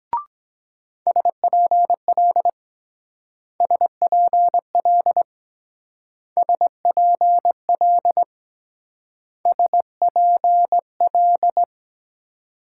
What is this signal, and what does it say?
Morse code, heard as a keyed tone, about 22 wpm, 700 Hz. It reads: SPL SPL SPL SPL